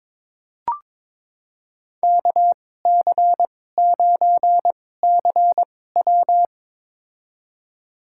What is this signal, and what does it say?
Morse code, heard as a keyed tone, about 22 wpm, spaced as for 16 wpm, 700 Hz. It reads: KC9CW
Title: USA Call Signs - 2x Character Spacing - 22wpm